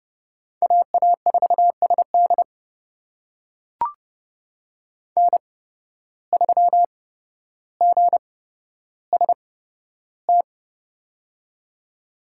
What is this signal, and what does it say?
Morse code, heard as a keyed tone, about 31 wpm, spaced as for 13 wpm, 700 Hz. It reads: AA4SD N3GST
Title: USA Call Signs - 5 Characters - 8x Character Spacing - 30wpm